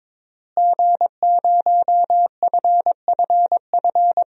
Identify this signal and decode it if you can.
Morse code, heard as a keyed tone, about 22 wpm, 700 Hz. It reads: G0FFF
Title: DXCC-100 Call Signs - 5 Characters - Repeated 3x - 22wpm